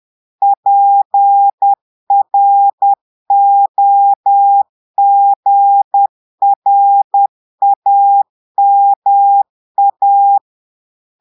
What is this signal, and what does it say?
Morse code, heard as a keyed tone, about 10 wpm, 800 Hz. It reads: PROGRAMA
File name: portugais texte en clair 2 vitesse 10 mots minutes